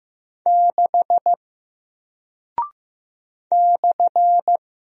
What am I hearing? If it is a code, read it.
Morse code, heard as a keyed tone, about 15 wpm, 700 Hz. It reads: 6 /
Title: Single Character (Letter:Number:Punctuation:Prosign) Alt 15wpm